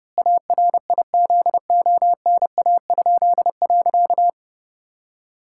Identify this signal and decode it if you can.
Morse code, heard as a keyed tone, about 30 wpm, 700 Hz. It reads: ARIZONA?.
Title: Pangram Challenge v2 - 30wpm